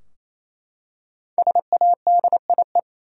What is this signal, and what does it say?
Morse code, heard as a keyed tone, about 27 wpm, 700 Hz. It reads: SADIE
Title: USA Common Names - 28wpm